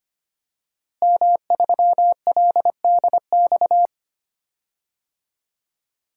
Morse code, heard as a keyed tone, about 25 wpm, 700 Hz. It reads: M3LDX
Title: DXCC-100 Call Signs - 5 Characters - Repeated 2x - 25wpm